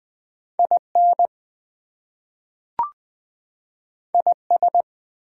Morse code, heard as a keyed tone, about 20 wpm, 700 Hz. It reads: IN IS
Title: Two Letter Words - 20wpm